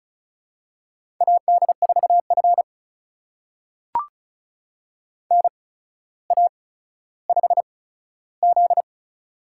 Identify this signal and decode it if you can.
Morse code, heard as a keyed tone, about 36 wpm, spaced as for 15 wpm, 700 Hz. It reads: AD4F NA5Z
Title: USA Call Signs - 4 Characters - 8x Character Spacing - 35wpm